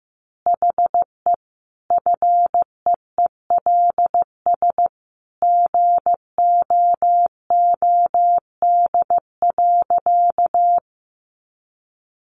Morse code, heard as a keyed tone, about 15 wpm, 700 Hz. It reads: HE FEELS GOOD.
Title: Additional Easy Phrases 15wpm